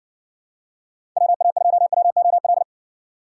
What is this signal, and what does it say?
Morse code, heard as a keyed tone, about 70 wpm, 700 Hz. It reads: WA2WOL